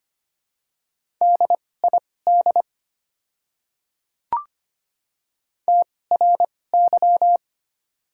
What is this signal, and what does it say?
Morse code, heard as a keyed tone, about 25 wpm, spaced as for 18 wpm, 700 Hz. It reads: DID TRY
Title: 2x Character Spacing - Top 200 Words - 25wpm